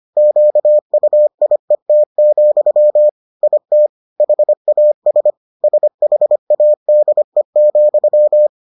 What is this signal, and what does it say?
Morse code, heard as a keyed tone, about 25 wpm, 600 Hz. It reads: QUIET, IT HAS SHADE,